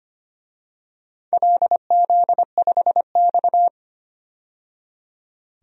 Morse code, heard as a keyed tone, about 25 wpm, 700 Hz. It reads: LZ5X
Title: DXCC-100 Call Signs - 4 Characters - 25wpm